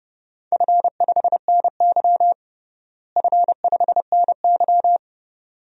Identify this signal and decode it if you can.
Morse code, heard as a keyed tone, about 30 wpm, 700 Hz. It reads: F5NY F5NY
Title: DXCC-100 Call Signs - 4 Characters - Repeated 2x - 30wpm